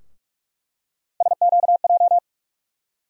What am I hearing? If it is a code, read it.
Morse code, heard as a keyed tone, about 46 wpm, 700 Hz. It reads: SQJ
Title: USA Call Sign Postfixes - 3 Letters 45wpm